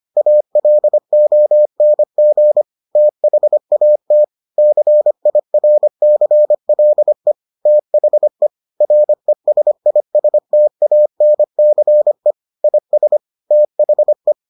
Morse code, heard as a keyed tone, about 25 wpm, 600 Hz. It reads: ALONG THAT CIRCLE THE RESISTANCE IS THE